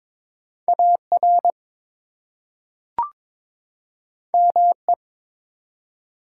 Morse code, heard as a keyed tone, about 22 wpm, 700 Hz. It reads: AR ME
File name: ARRL Sections v2 - 22wpm